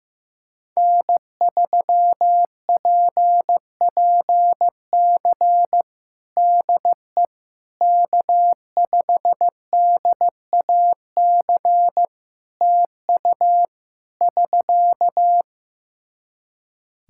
Morse code, heard as a keyed tone, about 16 wpm, 700 Hz. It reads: N3PPC DE K5DAC TU <SK>
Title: K1USN SST QSO Practice 15wpm